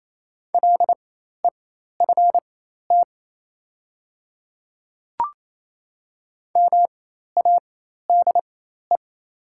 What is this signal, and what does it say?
Morse code, heard as a keyed tone, about 28 wpm, spaced as for 13 wpm, 700 Hz. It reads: LEFT MADE
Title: 4x Character Spacing - Top 300 Words - 28wpm